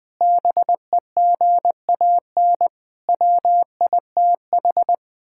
Morse code, heard as a keyed tone, about 21 wpm, 700 Hz. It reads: BEGAN WITH
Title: Sentences from Top 1600 Words 20wpm